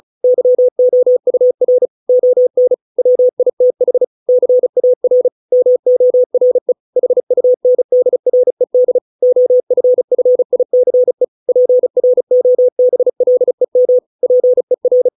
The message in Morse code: YOUR ON WITH CAR MORE HUNDRED OFFICE PROBLEM PER

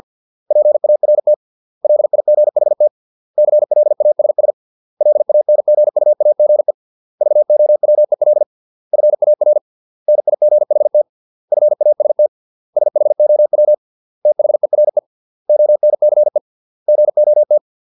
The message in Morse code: PART LIGHT CLASS LANGUAGE VOWEL FAR NIGHT FAST SHOW THERE ONCE GOT

